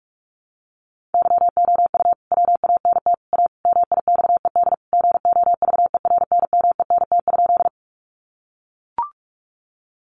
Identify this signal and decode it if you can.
Morse code, heard as a keyed tone, about 44 wpm, 700 Hz. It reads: YOU WANT A MIXED GOVERNMENT?
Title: Sentences from Top 1500 Words 45wpm